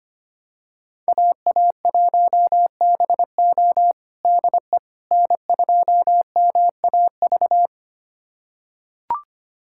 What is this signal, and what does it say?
Morse code, heard as a keyed tone, about 25 wpm, 700 Hz. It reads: AA1BO DE N2MAV
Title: USA Callsign DE Callsign - 5 Characters - 25wpm